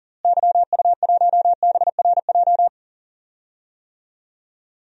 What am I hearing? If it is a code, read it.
Morse code, heard as a keyed tone, about 40 wpm, 700 Hz. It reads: YU1BRJ
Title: DXCC-100 Call Signs - 6 Characters - Repeated 2x - 40wpm